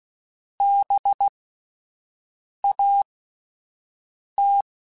Morse code, heard as a keyed tone, about 16 wpm, spaced as for 5 wpm, 800 Hz. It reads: BAT
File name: Spaced Alphas / Spaced Alphas mp3s / Bs sp @10